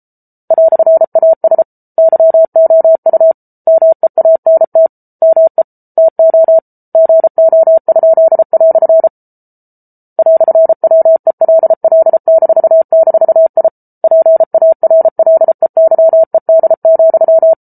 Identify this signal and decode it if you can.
Morse code, heard as a keyed tone, about 34 wpm, 650 Hz. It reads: "AS YOU MEANT ME TO GO?" "WELL--I PARLEYED,